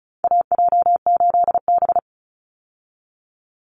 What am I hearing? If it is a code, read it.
Morse code, heard as a keyed tone, about 34 wpm, 700 Hz. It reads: AJ8B
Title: Call Signs Repeated 2x - 35wpm